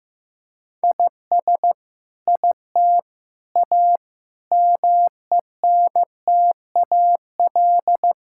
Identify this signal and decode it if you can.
Morse code, heard as a keyed tone, about 15 wpm, 700 Hz. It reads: IS IT A MENTAL
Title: Sentences from Top 1000 Words - Review Part 6 - 15wpm